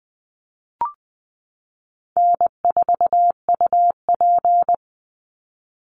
Morse code, heard as a keyed tone, about 20 wpm, 700 Hz. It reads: N4UP
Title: Easy Call Signs Repeated 2x - 20wpm